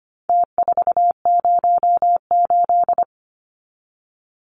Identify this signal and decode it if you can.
Morse code, heard as a keyed tone, about 25 wpm, 700 Hz. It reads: T408
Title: Numbers - Contest Style - 4-Digit 25wpm